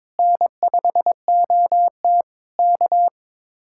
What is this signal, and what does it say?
Morse code, heard as a keyed tone, about 22 wpm, 700 Hz. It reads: N5OT K
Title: Speed-Racing Common QSO Phrases - 22wpm